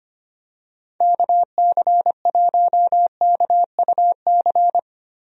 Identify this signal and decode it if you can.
Morse code, heard as a keyed tone, about 25 wpm, 700 Hz. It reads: KC1KUC
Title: USA Call Signs - 6 Characters - Repeated 3x - 25wpm